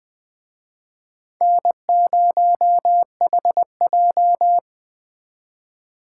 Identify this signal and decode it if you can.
Morse code, heard as a keyed tone, about 20 wpm, 700 Hz. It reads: N0HJ
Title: Speed Racing - USA 1x2 Call Signs 17wpm